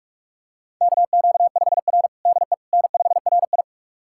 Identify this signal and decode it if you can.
Morse code, heard as a keyed tone, about 46 wpm, 700 Hz. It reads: KQ5R DE N5RI
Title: USA Callsign DE Callsign - 4 Characters - 45wpm